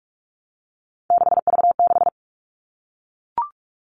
Morse code, heard as a keyed tone, about 44 wpm, 700 Hz. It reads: 6V6
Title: All Cut Numbers - 3-Digit 45wpm